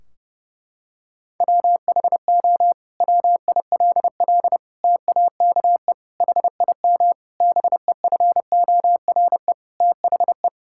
Morse code, heard as a keyed tone, about 30 wpm, 700 Hz. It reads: WHO WILL TAKE HIM BEFORE THE
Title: Sentences from Top 1600 Words 30wpm